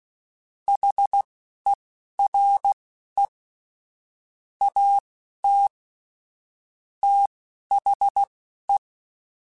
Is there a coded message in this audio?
Yes: Morse code from 0.7 to 8.8 s, heard as a keyed tone, about 16 wpm, spaced as for 11 wpm, 800 Hz. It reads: HERE AT THE